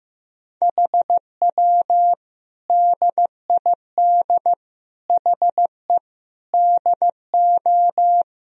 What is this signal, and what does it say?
Morse code, heard as a keyed tone, about 15 wpm, 700 Hz. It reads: HW DID HE DO
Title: Sentences from Top 100 Words and Abbreviations 15wpm